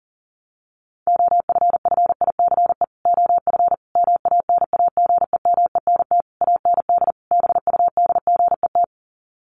Morse code, heard as a keyed tone, about 40 wpm, 700 Hz. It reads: OFFICE OF MANAGEMENT AND BUDGET
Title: Sets of 5 Words Encore2 40wpm